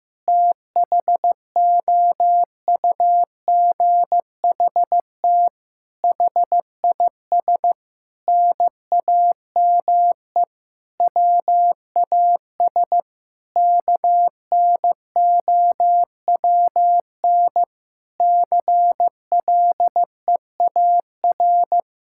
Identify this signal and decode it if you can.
Morse code, heard as a keyed tone, about 15 wpm, 700 Hz. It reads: THOUGHT HIS NAME WAS KNOWN CLEAR
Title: Sentences from Top 1000 Words - Review Part 5 - 15wpm